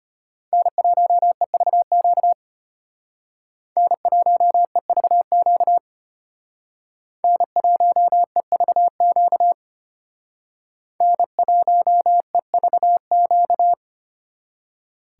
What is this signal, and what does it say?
Morse code, heard as a keyed tone, about 32 wpm, 700 Hz. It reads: N1EVQ N1EVQ N1EVQ N1EVQ